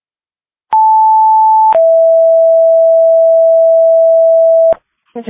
Two-tone page, heard as a one-second tone, then a three-second tone, about 880 then 660 Hz. It is heard from 0.7 to 4.7 s.